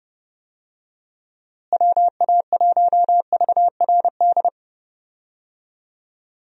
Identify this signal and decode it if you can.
Morse code, heard as a keyed tone, about 29 wpm, 700 Hz. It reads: WA1VRD
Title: USA Call Signs - 6 Characters - Repeated 3x - 30wpm